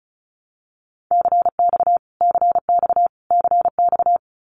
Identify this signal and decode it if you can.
Morse code, heard as a keyed tone, about 34 wpm, 700 Hz. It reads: CX CX CX
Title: Abbreviations Repeated 3x - 35wpm